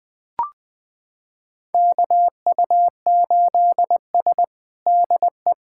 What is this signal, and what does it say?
Morse code, heard as a keyed tone, about 20 wpm, 700 Hz. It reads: KU8S DE
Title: USA Callsign DE Callsign 20wpm